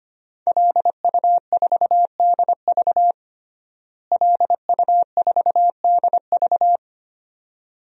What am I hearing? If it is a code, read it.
Morse code, heard as a keyed tone, about 25 wpm, 700 Hz. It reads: LU4DV LU4DV
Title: DXCC-100 Call Signs - 5 Characters - Repeated 2x - 25wpm